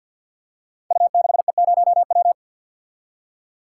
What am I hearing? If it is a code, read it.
Morse code, heard as a keyed tone, about 46 wpm, 700 Hz. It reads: U6E0W
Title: Code Groups - 5 Letter:Numbers - 50wpm